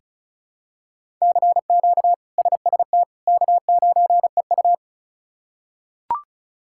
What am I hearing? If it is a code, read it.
Morse code, heard as a keyed tone, about 36 wpm, 700 Hz. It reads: CQ SST K9EU
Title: K1USN SST QSO Practice - 35wpm